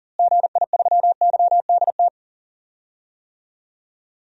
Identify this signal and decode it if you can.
Morse code, heard as a keyed tone, about 37 wpm, 700 Hz. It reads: GI3YDT
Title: DXCC-100 Call Signs - 6 Characters - Repeated 2x - 40wpm